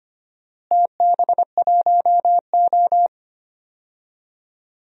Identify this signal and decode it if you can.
Morse code, heard as a keyed tone, about 25 wpm, 700 Hz. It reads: TB1O
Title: DXCC-100 Call Signs - 4 Characters - Repeated 3x - 25wpm